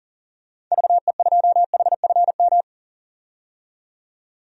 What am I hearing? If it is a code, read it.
Morse code, heard as a keyed tone, about 37 wpm, 700 Hz. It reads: VE2HFM
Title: DXCC-100 Call Signs - 6 Characters - 40wpm